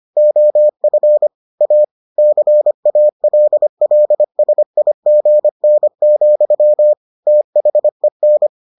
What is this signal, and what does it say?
Morse code, heard as a keyed tone, about 25 wpm, 600 Hz. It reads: OF A CALLSIGN, THEN